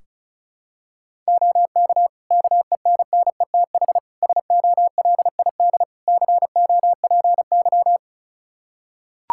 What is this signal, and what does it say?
Morse code, heard as a keyed tone, about 36 wpm, 700 Hz. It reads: OK KENNETH SOLID COPY
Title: Practice QSO - Exchange 3 - More Info - 35wpm